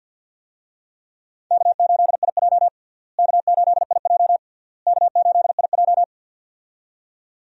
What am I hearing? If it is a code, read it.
Morse code, heard as a keyed tone, about 51 wpm, 700 Hz. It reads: K8IJ K8IJ K8IJ
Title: USA Call Signs - 4 Characters - Repeated 3x - 50wpm